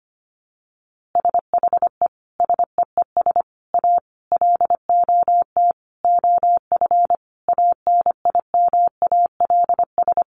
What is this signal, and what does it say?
Morse code, heard as a keyed tone, about 25 wpm, 700 Hz. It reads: SHE SEES A LOT OF ANIMALS
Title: Easy Phrases 25wpm